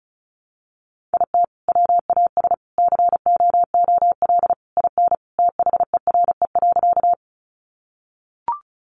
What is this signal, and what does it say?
Morse code, heard as a keyed tone, about 34 wpm, 700 Hz. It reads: IT WAS COOL IN THERE.
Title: Sentences from Top 1200 Words 35wpm